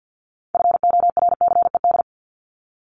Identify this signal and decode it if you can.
Morse code, heard as a keyed tone, about 50 wpm, 700 Hz. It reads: FORCED
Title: ED Suffix - Top 100 Words with Base Words 50wpm